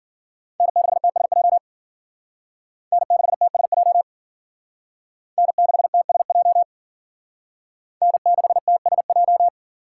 Morse code, heard as a keyed tone, about 53 wpm, 700 Hz. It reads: N6TSJ N6TSJ N6TSJ N6TSJ